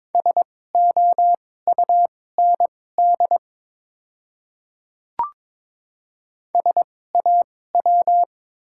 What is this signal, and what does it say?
Morse code, heard as a keyed tone, about 22 wpm, spaced as for 16 wpm, 700 Hz. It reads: SOUND SAW
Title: 2x Character Spacing - Top 300 Words - 22wpm